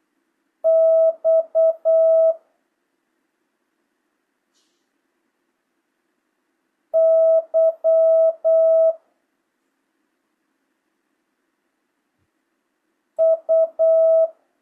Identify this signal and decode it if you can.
Morse code, heard as a keyed tone, about 8 wpm, 650 Hz. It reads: X Y U
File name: jh8cbh / 2025cw